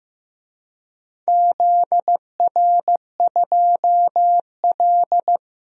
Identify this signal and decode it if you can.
Morse code, heard as a keyed tone, about 15 wpm, 700 Hz. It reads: ZR2L
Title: DXCC-100 Call Signs - 4 Characters - 15wpm